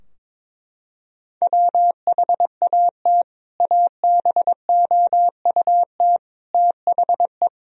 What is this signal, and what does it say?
Morse code, heard as a keyed tone, about 22 wpm, 700 Hz. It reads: WHAT ABOUT THE
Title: Sentences from Top 1000 Words - Review Part 2 - 22wpm